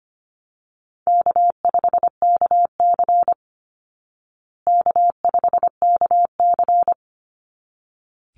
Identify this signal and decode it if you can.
Morse code, heard as a keyed tone, about 25 wpm, 700 Hz. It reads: K5KC K5KC
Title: Easy Call Signs Repeated 2x - 25wpm